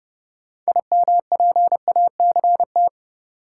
Morse code, heard as a keyed tone, about 30 wpm, 700 Hz. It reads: IMPACT
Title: Sentences from Top 1000 Words - Review Part 6 - New Word Drill - 30wpm